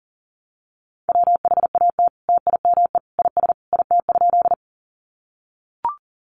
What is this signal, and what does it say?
Morse code, heard as a keyed tone, about 40 wpm, 700 Hz. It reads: WHAT TIME IS IT?
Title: Sentences from Top 100 Words 40wpm